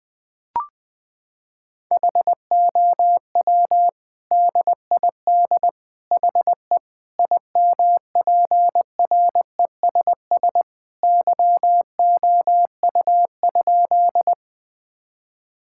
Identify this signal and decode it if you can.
Morse code, heard as a keyed tone, about 20 wpm, 700 Hz. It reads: HOW DID HE IMPRESS YOU?
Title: Sentences from Top 1600 Words 20wpm